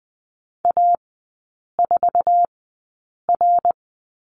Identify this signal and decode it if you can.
Morse code, heard as a keyed tone, about 20 wpm, spaced as for 8 wpm, 700 Hz. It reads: A4R
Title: 074 Lesson 13e - Letter L - Three Characters 20wpm